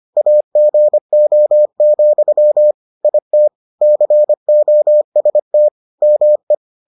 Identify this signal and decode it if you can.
Morse code, heard as a keyed tone, about 25 wpm, 600 Hz. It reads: AGO, IT COST ME